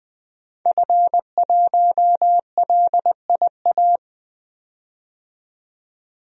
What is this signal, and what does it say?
Morse code, heard as a keyed tone, about 20 wpm, 700 Hz. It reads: F1LIA
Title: DXCC-100 Call Signs - 5 Characters - 20wpm